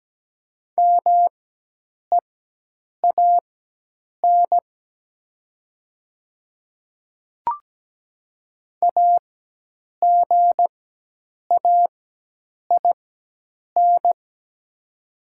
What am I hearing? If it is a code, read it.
Morse code, heard as a keyed tone, about 17 wpm, spaced as for 8 wpm, 700 Hz. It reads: MEAN AGAIN
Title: 4x Character Spacing - Top 200 Words - 17wpm